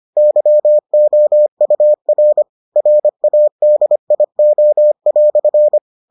Morse code, heard as a keyed tone, about 25 wpm, 600 Hz. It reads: YOUR RADIO"